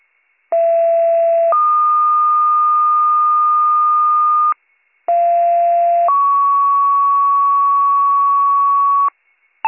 Two-tone page, heard as a one-second tone, then a three-second tone, about 670 then 1220 Hz, about 690 then 1120 Hz. It is heard from 0.5 to 9.1 s.